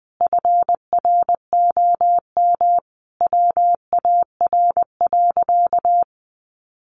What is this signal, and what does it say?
Morse code, heard as a keyed tone, about 20 wpm, 700 Hz. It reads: FROM WAR.
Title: Sentences from Top 1500 Words 20wpm